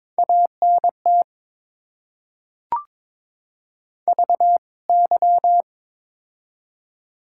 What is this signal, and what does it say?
Morse code, heard as a keyed tone, about 22 wpm, 700 Hz. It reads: ANT VY
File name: Top 90 QSO Elements - 2x Character Spacing - 22wpm